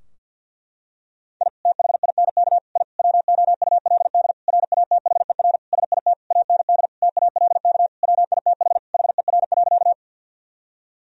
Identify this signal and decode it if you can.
Morse code, heard as a keyed tone, about 46 wpm, 700 Hz. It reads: I THINK I WOULD RATHER SIT AND TALK WITH HER.
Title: Sentences from Top 1000 Words - Review Part 4 - 50wpm